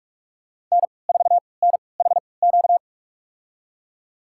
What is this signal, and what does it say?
Morse code, heard as a keyed tone, about 46 wpm, spaced as for 25 wpm, 700 Hz. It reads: N4NHQ
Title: USA Call Signs - 5 Characters - 3x Character Spacing - 45wpm